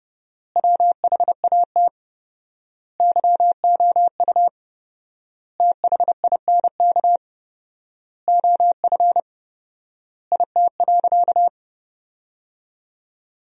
Morse code, heard as a keyed tone, about 30 wpm, 700 Hz. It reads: WHAT YOU THINK OF IT.
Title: 4x Word-Spacing - Sentences from Top 200 Words - 30wpm